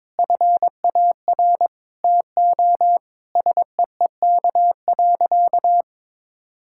Morse code, heard as a keyed tone, about 22 wpm, 700 Hz. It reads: FAR TO SEEK.
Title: Sentences from Top 1600 Words 22wpm